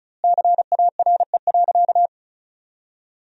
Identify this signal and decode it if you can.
Morse code, heard as a keyed tone, about 34 wpm, 700 Hz. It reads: CARE.
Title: 2x Word-Spacing - Sentences from Top 300 Words - 35wpm